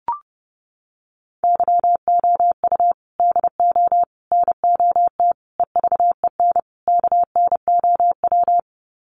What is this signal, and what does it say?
Morse code, heard as a keyed tone, about 30 wpm, 700 Hz. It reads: YOU DO NOT EVEN KNOW
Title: Sets of 5 Words Encore2 30wpm